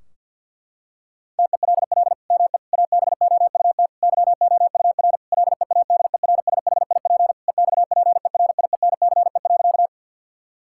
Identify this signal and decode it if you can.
Morse code, heard as a keyed tone, about 51 wpm, 700 Hz. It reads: TELL ME ABOUT YOUR LEADERSHIP EXPERIENCE.
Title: Sentences from Top 1600 Words 50wpm